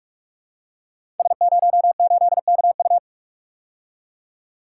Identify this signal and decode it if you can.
Morse code, heard as a keyed tone, about 42 wpm, 700 Hz. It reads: S08KU
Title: Code Groups - 5 Letter:Numbers - 45wpm